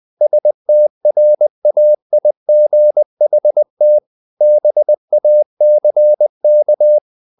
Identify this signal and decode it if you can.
Morse code, heard as a keyed tone, about 20 wpm, 600 Hz. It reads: STRAIGHT BACK
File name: the old man and the sea part 2 20wpm